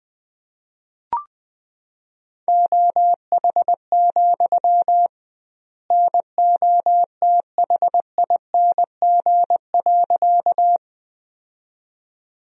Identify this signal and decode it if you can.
Morse code, heard as a keyed tone, about 20 wpm, 700 Hz. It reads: OH, NOTHING.
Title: 2x Word-Spacing - Sentences from Top 500 Words - 20wpm